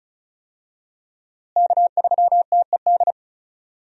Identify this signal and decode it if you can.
Morse code, heard as a keyed tone, about 36 wpm, 700 Hz. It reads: K3TED
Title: USA Call Signs - 5 Characters - Repeated 3x - 35wpm